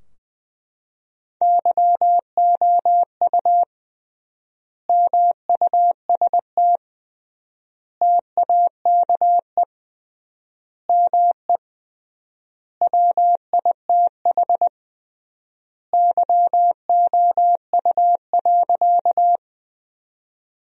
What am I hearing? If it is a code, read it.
Morse code, heard as a keyed tone, about 20 wpm, 700 Hz. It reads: YOU MUST TAKE ME WITH YOU.
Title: 3x Word-Spacing - Sentences from Top 200 Words - 20wpm